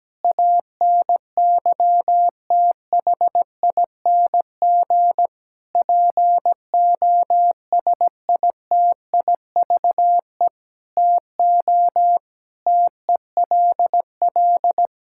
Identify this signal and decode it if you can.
Morse code, heard as a keyed tone, about 17 wpm, 700 Hz. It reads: ANYTHING POSITIVE TO TELL
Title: Sentences from Top 1000 Words - Review Part 5 - 17wpm